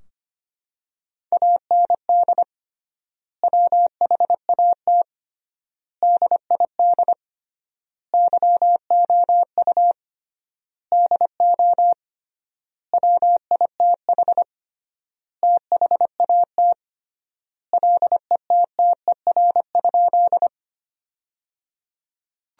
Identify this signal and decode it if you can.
Morse code, heard as a keyed tone, about 25 wpm, 700 Hz. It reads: AND WHAT DID YOU DO WITH THAT LETTER?
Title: 3x Word-Spacing - Sentences from Top 300 Words - 25wpm